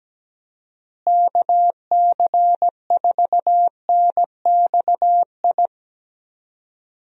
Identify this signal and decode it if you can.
Morse code, heard as a keyed tone, about 17 wpm, 700 Hz. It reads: KC4NXI
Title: USA Call Signs - 6 Characters - Repeated 2x - 17wpm